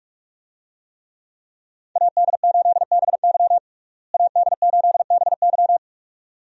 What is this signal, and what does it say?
Morse code, heard as a keyed tone, about 46 wpm, 700 Hz. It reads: AD8BY AD8BY
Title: USA Call Signs - 5 Characters - Repeated 2x - 45wpm